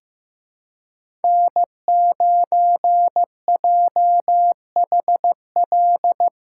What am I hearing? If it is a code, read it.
Morse code, heard as a keyed tone, about 15 wpm, 700 Hz. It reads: N9JHL
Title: K1USN SST QSO Practice - 15wpm